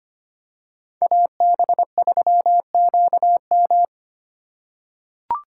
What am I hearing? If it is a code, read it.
Morse code, heard as a keyed tone, about 25 wpm, 700 Hz. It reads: AB3QM
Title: USA Call Signs - 5 Characters - 3x Character Spacing - 25wpm